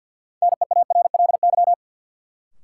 Morse code, heard as a keyed tone, about 53 wpm, 700 Hz. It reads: NEARLY
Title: Sentences from Top 1300 Words - New Word Drill Alt - 50wpm